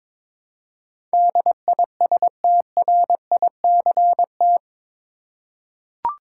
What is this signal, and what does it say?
Morse code, heard as a keyed tone, about 22 wpm, 700 Hz. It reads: DISTRICT
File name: practice / 22wpm / Eight Letter Words - 22wpm